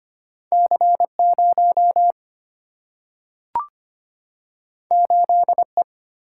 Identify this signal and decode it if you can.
Morse code, heard as a keyed tone, about 25 wpm, 700 Hz. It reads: C 0 8 E
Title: Single-Letter Single-Number 25wpm